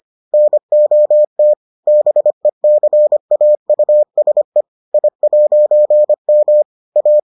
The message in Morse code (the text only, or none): NOT BECAUSE I'M A